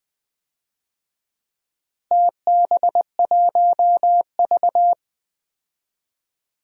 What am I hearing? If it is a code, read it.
Morse code, heard as a keyed tone, about 20 wpm, 700 Hz. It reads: TB1V
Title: DXCC-100 Call Signs - 4 Characters - Repeated 2x - 20wpm